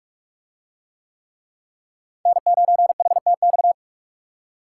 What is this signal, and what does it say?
Morse code, heard as a keyed tone, about 41 wpm, 700 Hz. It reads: N9HTX